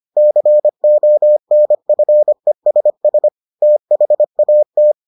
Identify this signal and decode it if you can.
Morse code, heard as a keyed tone, about 25 wpm, 600 Hz. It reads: CONFESS THAT